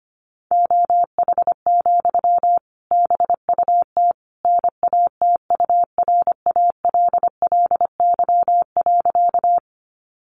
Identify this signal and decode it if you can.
Morse code, heard as a keyed tone, about 25 wpm, 700 Hz. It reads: OH, BUT NATURALLY.
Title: Sentences from Top 1300 Words 25wpm